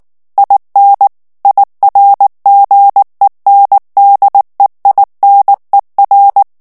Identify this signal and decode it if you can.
Morse code, heard as a keyed tone, about 19 wpm, 800 Hz. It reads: IN IRGENDEINER
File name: morse mp3 / ALLEMAND texte clair 2 vitesse 18 mots minute